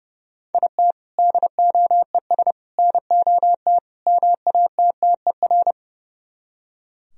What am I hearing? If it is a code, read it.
Morse code, heard as a keyed tone, about 37 wpm, 700 Hz. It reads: IT DOES NOT MATTER
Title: Speed-Racing - Sets of 4 Words 30wpm